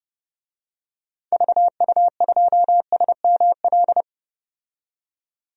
Morse code, heard as a keyed tone, about 31 wpm, 700 Hz. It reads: VU2SML
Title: DXCC-100 Call Signs - 6 Characters - 30wpm